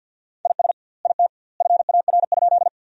Morse code, heard as a keyed tone, about 46 wpm, 700 Hz. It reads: IS IT FAR?
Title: 2x Word-Spacing - Sentences from Top 300 Words - 50wpm